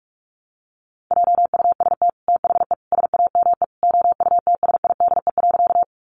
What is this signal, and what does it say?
Morse code, heard as a keyed tone, about 44 wpm, 700 Hz. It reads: JUST THE SAME OUTSIDE.
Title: Sentences from Top 900 Words 45wpm